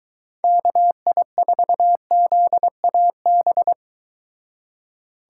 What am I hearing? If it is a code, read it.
Morse code, heard as a keyed tone, about 23 wpm, 700 Hz. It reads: KI4ZAB